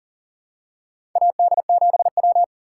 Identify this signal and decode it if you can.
Morse code, heard as a keyed tone, about 40 wpm, 700 Hz. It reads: AD7W
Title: USA Call Signs - 4 Characters - 3x Character Spacing - 40wpm